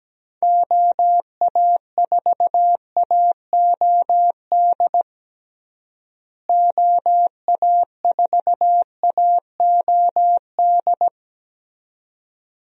Morse code, heard as a keyed tone, about 17 wpm, 700 Hz. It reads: OA4AOD OA4AOD
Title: USA Call Signs - 6 Characters - Repeated 2x - 17wpm